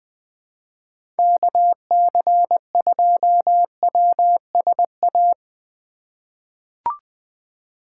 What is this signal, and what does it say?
Morse code, heard as a keyed tone, about 21 wpm, 700 Hz. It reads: KC2WSA
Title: USA Call Signs - 2x Character Spacing - 20wpm